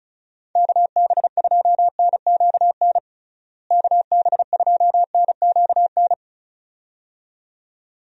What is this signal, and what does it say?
Morse code, heard as a keyed tone, about 34 wpm, 700 Hz. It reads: KB2NQN KB2NQN
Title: USA Call Signs - 6 Characters - Repeated 2x - 35wpm